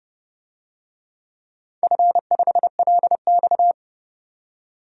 Morse code, heard as a keyed tone, about 30 wpm, 700 Hz. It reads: F5LX
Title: DXCC-100 Call Signs - 4 Characters - 30wpm